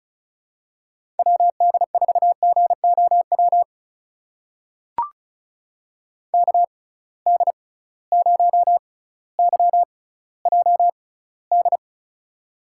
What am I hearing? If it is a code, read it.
Morse code, heard as a keyed tone, about 34 wpm, spaced as for 18 wpm, 700 Hz. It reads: WD4GOW KD0YJD
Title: USA Call Signs - 6x Character Spacing - 35wpm